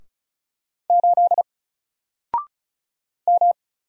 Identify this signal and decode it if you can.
Morse code, heard as a keyed tone, about 36 wpm, 700 Hz. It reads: 8 M
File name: Single-Letter-Number-35wpm